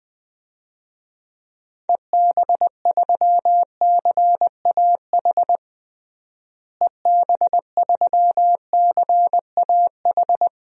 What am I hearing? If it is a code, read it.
Morse code, heard as a keyed tone, about 20 wpm, 700 Hz. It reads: EB3CAH EB3CAH
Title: DXCC-100 Call Signs - 6 Characters - Repeated 2x - 20wpm